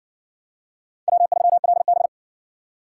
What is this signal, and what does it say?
Morse code, heard as a keyed tone, about 58 wpm, 700 Hz. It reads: W3LB